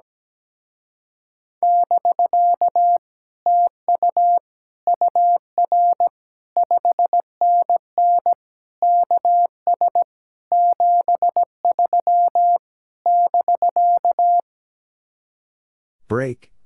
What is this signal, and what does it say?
Morse code, heard as a keyed tone, about 17 wpm, 700 Hz. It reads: <BK> TU UR 5NN KS 73 <BK>
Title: Speed-Racing Common QSO Phrases - 17wpm